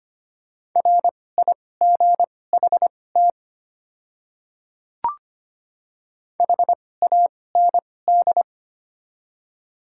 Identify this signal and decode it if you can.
Morse code, heard as a keyed tone, about 25 wpm, spaced as for 18 wpm, 700 Hz. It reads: RIGHT HAND
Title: 2x Character Spacing - Top 300 Words - 25wpm